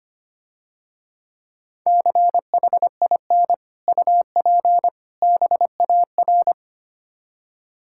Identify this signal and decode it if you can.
Morse code, heard as a keyed tone, about 28 wpm, 700 Hz. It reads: CHIN UP BAR
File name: practice / 25wpm / Difficult Words - Speed-Racing with Hints - 25wpm